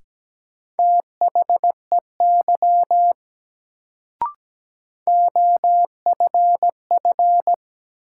Morse code, heard as a keyed tone, about 17 wpm, 700 Hz. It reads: THEY OFF
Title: Sets of 2 Words - New Word Drill - 17wpm